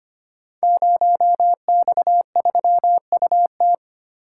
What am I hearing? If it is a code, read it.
Morse code, heard as a keyed tone, about 25 wpm, 700 Hz. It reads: 0X3UT
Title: Code Groups - 5 Letter:Numbers - 25wpm